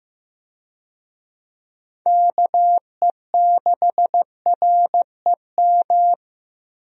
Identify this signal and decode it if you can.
Morse code, heard as a keyed tone, about 16 wpm, 700 Hz. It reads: KE6REM
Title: USA Call Signs - 6 Characters - Repeated 2x - 15wpm